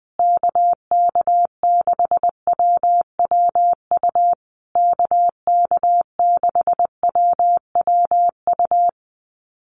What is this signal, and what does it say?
Morse code, heard as a keyed tone, about 20 wpm, 700 Hz. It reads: KK6WWU KK6WWU
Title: USA Call Signs Repeated 3x 20wpm